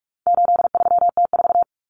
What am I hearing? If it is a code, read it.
Morse code, heard as a keyed tone, about 44 wpm, 700 Hz. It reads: 83T4
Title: Common Cut Numbers - 4-Digit 45wpm